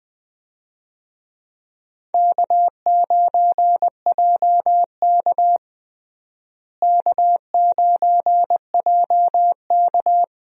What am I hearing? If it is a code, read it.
Morse code, heard as a keyed tone, about 20 wpm, 700 Hz. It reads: K9JK K9JK
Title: USA Call Signs - 4 Characters - Repeated 2x - 20wpm